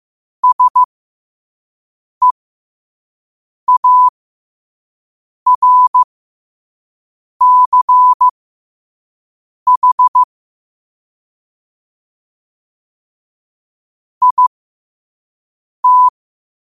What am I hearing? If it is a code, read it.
Morse code, heard as a keyed tone, about 15 wpm, spaced as for 5 wpm, 1000 Hz. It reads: SEARCH IT
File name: qotdfiles / 5wpm